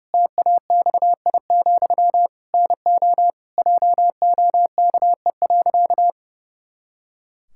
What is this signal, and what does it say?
Morse code, heard as a keyed tone, about 30 wpm, 700 Hz. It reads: TAXI, NO JOKE.
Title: Pangram Challenge v2 - 30wpm